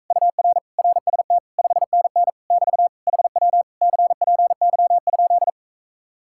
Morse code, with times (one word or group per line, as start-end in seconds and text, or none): 0.1-0.6 UR
0.8-1.4 RST
1.6-2.3 5NN
2.5-2.9 <BT>
3.1-3.6 HW
3.8-5.5 CPY?